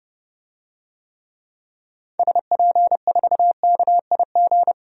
Morse code, heard as a keyed tone, about 30 wpm, 700 Hz. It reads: SP4KIG
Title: DXCC-100 Call Signs - 6 Characters - 30wpm